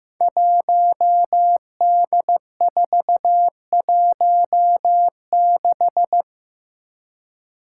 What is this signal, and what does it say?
Morse code, heard as a keyed tone, about 15 wpm, 700 Hz. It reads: 1D416
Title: Five Challenging Letter-Number Combinations 15wpm